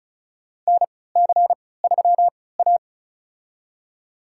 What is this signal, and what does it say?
Morse code, heard as a keyed tone, about 36 wpm, spaced as for 20 wpm, 700 Hz. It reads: NC3A
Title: USA Call Signs - 4 Characters - 3x Character Spacing - 35wpm